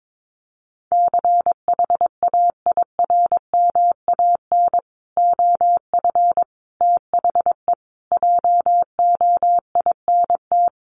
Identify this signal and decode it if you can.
Morse code, heard as a keyed tone, about 22 wpm, 700 Hz. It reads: CHAIRMAN OF THE JOINT
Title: Sets of 5 Words Encore 22wpm